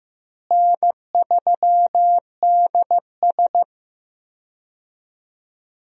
Morse code, heard as a keyed tone, about 15 wpm, 700 Hz. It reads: N3DS
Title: USA Call Signs - 4 Characters - Repeated 2x - 15wpm